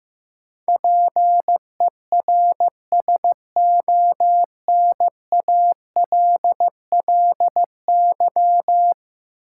Morse code, heard as a keyed tone, about 15 wpm, 700 Hz. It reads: PERSONALLY
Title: Sentences from Top 1400 Words - New Word Drill Alt - 15wpm